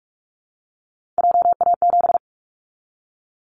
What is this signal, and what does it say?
Morse code, heard as a keyed tone, about 46 wpm, 700 Hz. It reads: JA7
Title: Callsign Prefixes for Top 50 DXCC - 45wpm